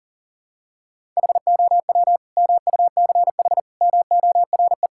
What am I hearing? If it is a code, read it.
Morse code, heard as a keyed tone, about 40 wpm, 700 Hz. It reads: HOW MUCH MORE
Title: Sentences from Top 1000 Words - Review Part 5 - 40wpm